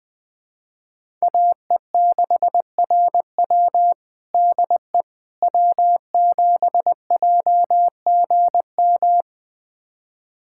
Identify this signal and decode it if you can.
Morse code, heard as a keyed tone, about 20 wpm, 700 Hz. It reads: AE6RW DE W7JGM
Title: USA Callsign DE Callsign - 5 Characters - 20wpm